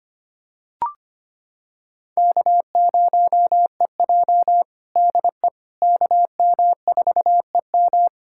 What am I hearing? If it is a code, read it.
Morse code, heard as a keyed tone, about 25 wpm, 700 Hz. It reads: K0EJ DE KM4EM
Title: Practice QSO - Exchange 4 - Ending - 25wpm